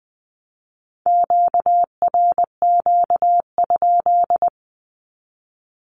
Morse code, heard as a keyed tone, about 20 wpm, 700 Hz. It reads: QRQ?
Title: Q-Codes with Spoken Meaning - 20wpm